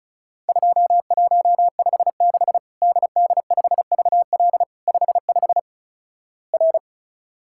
Morse code, heard as a keyed tone, about 34 wpm, 700 Hz. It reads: 2156 DD5VL 55 R
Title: Worked All Europe DX Contest - QTC Exchange - 35wpm